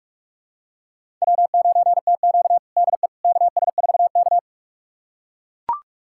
Morse code, heard as a keyed tone, about 44 wpm, 690 Hz. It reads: W9TQ DE KS4K
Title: USA Callsign DE Callsign - 4 Characters - 45wpm